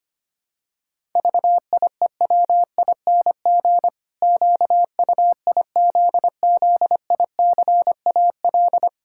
Morse code, heard as a keyed tone, about 25 wpm, 700 Hz. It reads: VIEWING QUIZZICAL
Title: Pangram Challenge v2 - 25wpm